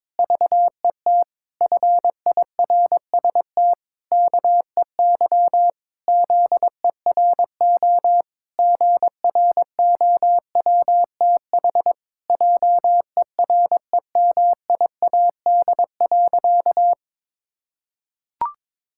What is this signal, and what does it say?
Morse code, heard as a keyed tone, about 22 wpm, 700 Hz. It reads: VET FIRST KEY ZERO GROWTH JEREMIAD.
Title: Pangram Challenge v2 - 22wpm